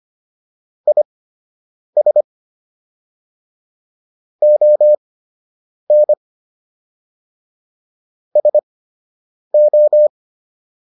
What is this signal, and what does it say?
Morse code, heard as a keyed tone, about 25 wpm, spaced as for 8 wpm, 600 Hz. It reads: IS ON SO